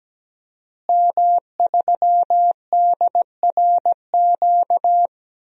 Morse code, heard as a keyed tone, about 17 wpm, 700 Hz. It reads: M3DRQ
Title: DXCC-100 Call Signs - 5 Characters - Repeated 2x - 17wpm